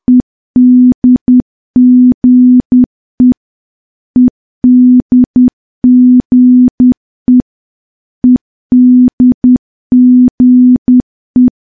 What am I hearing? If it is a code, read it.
Morse code, heard as a keyed tone, about 10 wpm, 250 Hz. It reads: EDGE EDGE EDGE
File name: morse / morse-training-10WPM